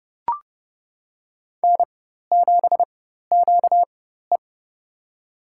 Morse code, heard as a keyed tone, about 30 wpm, spaced as for 14 wpm, 700 Hz. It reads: N7QE
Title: USA Call Signs - 4 Characters - 4x Character Spacing - 30wpm